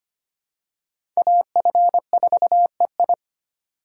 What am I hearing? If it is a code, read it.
Morse code, heard as a keyed tone, about 25 wpm, 700 Hz. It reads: AF4EI
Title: USA Call Signs - 5 Characters - 3x Character Spacing - 25wpm